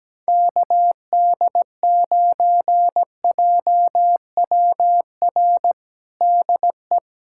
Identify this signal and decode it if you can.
Morse code, heard as a keyed tone, about 17 wpm, 700 Hz. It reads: KD9JWR DE
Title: USA Callsign DE Callsign 17wpm